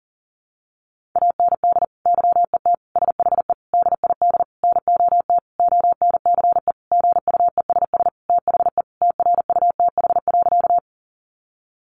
Morse code, heard as a keyed tone, about 40 wpm, 700 Hz. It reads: AND YET SHE DID NOT ONCE GUESS THE TRUTH.
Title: Sentences from Top 1500 Words 40wpm